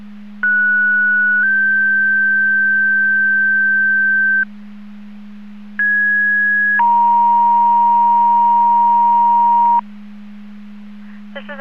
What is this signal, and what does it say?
Two-tone page, heard as a one-second tone, then a three-second tone, about 1470 then 1550 Hz, about 1650 then 940 Hz. It is heard from 0.4 to 9.8 s.